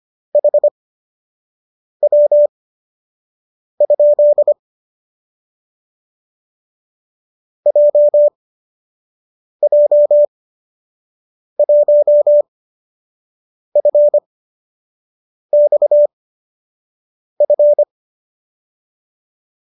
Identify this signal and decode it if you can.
Morse code, heard as a keyed tone, about 25 wpm, spaced as for 6 wpm, 590 Hz. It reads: HW? JJ1FXF